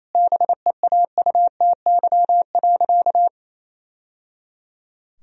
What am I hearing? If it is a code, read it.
Morse code, heard as a keyed tone, about 28 wpm, 700 Hz. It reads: BEAUTY.
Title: Sentences from Top 1000 Words - Review Part 2 - 28wpm